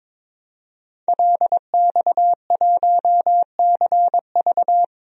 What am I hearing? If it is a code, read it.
Morse code, heard as a keyed tone, about 22 wpm, 700 Hz. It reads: LX1CV
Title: DXCC-100 Call Signs - 5 Characters - 22wpm